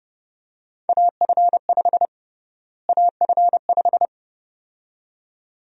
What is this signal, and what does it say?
Morse code, heard as a keyed tone, about 30 wpm, 700 Hz. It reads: AF5 AF5
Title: USA Call Sign Prefixes - Repeated 2x - 30wpm